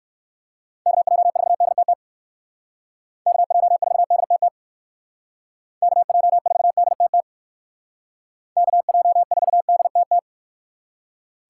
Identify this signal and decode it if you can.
Morse code, heard as a keyed tone, about 58 wpm, 700 Hz. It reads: KJ4DTT KJ4DTT KJ4DTT KJ4DTT